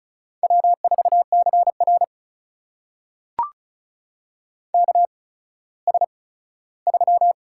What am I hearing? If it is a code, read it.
Morse code, heard as a keyed tone, about 36 wpm, spaced as for 17 wpm, 700 Hz. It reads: W4CR KS3
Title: USA Call Signs - 4 Characters - 8x Character Spacing - 35wpm